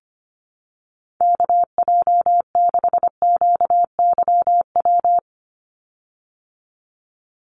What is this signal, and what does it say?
Morse code, heard as a keyed tone, about 25 wpm, 700 Hz. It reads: KJ6QYW
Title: USA Call Signs 25wpm